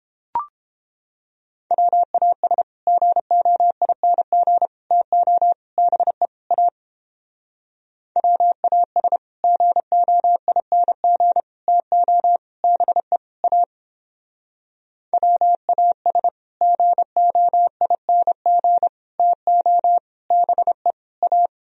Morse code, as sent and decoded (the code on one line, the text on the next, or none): .-- .- ... / --. --- .. -. --. / - --- / -... / .- / .-- .- ... / --. --- .. -. --. / - --- / -... / .- / .-- .- ... / --. --- .. -. --. / - --- / -... / .-
WAS GOING TO BE A WAS GOING TO BE A WAS GOING TO BE A